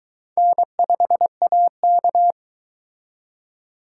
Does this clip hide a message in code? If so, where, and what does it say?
Yes: Morse code from 0.4 to 2.3 s, heard as a keyed tone, about 23 wpm, 700 Hz. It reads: N5AK